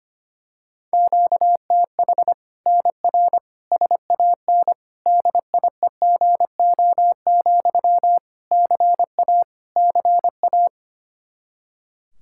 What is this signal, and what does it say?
Morse code, heard as a keyed tone, about 25 wpm, 700 Hz. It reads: QTH NR SAN DIEGO, CA CA
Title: Practice QSO - Exchange 1 - Beginning - 25wpm